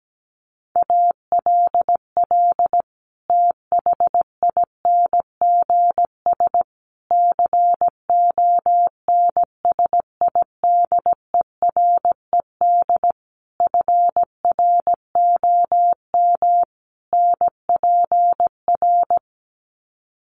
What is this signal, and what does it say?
Morse code, heard as a keyed tone, about 17 wpm, 700 Hz. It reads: ALL THINGS CONSIDERED FROM NPR
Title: Sets of 5 Words Encore 17wpm